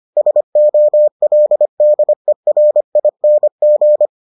Morse code, heard as a keyed tone, about 25 wpm, 600 Hz. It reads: SOLDERING